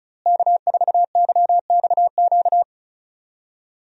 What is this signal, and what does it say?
Morse code, heard as a keyed tone, about 36 wpm, 700 Hz. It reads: K4YXQ
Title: USA Call Signs - 5 Characters - Repeated 3x - 35wpm